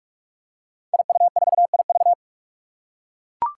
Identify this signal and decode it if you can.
Morse code, heard as a keyed tone, about 44 wpm, 700 Hz. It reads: IU4IV
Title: Five Challenging Letter-Number Combinations 45wpm